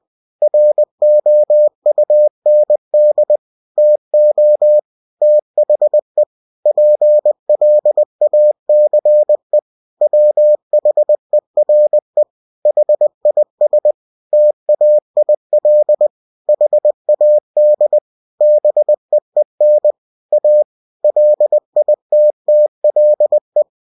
Morse code, heard as a keyed tone, about 20 wpm, 600 Hz. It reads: ROUND TO THE PLACE WHERE HIS TAIL HAD BEEN A LITTLE
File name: cw-books / winnie-the-pooh / winnie-20wpm-page-0027